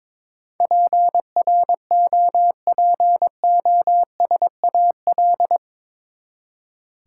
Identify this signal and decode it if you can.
Morse code, heard as a keyed tone, about 23 wpm, 700 Hz. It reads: PROPOSAL
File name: practice / 22wpm / Eight Letter Words - 22wpm